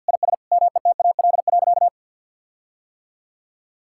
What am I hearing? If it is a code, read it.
Morse code, heard as a keyed tone, about 46 wpm, 700 Hz. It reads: IS METAL.
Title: Sentences from Top 1600 Words 50wpm